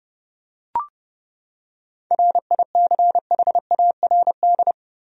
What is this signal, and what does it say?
Morse code, heard as a keyed tone, about 30 wpm, 700 Hz. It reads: RICHARD
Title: USA Common Names - 30wpm